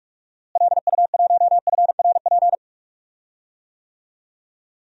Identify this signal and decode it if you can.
Morse code, heard as a keyed tone, about 41 wpm, 700 Hz. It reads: LU1FRP
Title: DXCC-100 Call Signs - 6 Characters - Repeated 2x - 45wpm